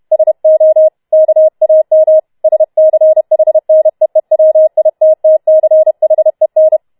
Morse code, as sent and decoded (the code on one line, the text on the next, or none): ... --- / -.- .- -- / ... -.-. .... -. . .-- .. - - -.-. .... -.
SO KAM SCHNEEWITTCHEN